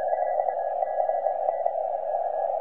Morse code, heard as a keyed tone, about 27 wpm, 1950 Hz, with radio noise. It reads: L CN
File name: S21XV 40CW